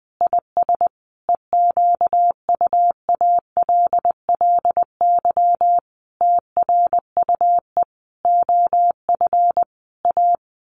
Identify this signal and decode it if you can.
Morse code, heard as a keyed tone, about 20 wpm, 700 Hz. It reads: IS EQUALLY TRUE OF A
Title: Sentences from Top 1400 Words 20wpm